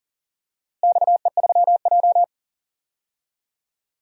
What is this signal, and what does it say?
Morse code, heard as a keyed tone, about 40 wpm, 690 Hz. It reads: XE3J
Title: DXCC-100 Call Signs - 4 Characters - Repeated 2x - 40wpm